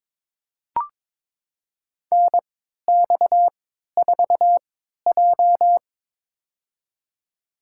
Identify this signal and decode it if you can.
Morse code, heard as a keyed tone, about 22 wpm, spaced as for 13 wpm, 690 Hz. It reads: NX4J
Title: USA Call Signs - 4 Characters - 3x Character Spacing - 22wpm